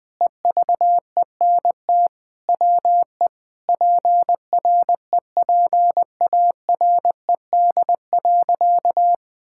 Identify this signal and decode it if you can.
Morse code, heard as a keyed tone, about 21 wpm, 700 Hz. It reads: EVENT WE PREPARED.
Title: Sentences with 6-Word Phrases - 20wpm